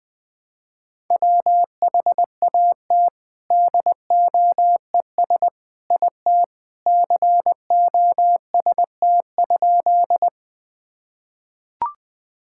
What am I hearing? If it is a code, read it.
Morse code, heard as a keyed tone, about 21 wpm, 700 Hz. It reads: WHAT DOES IT COST?
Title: Sentences from Top 1000 Words - Review Part 5 - 20wpm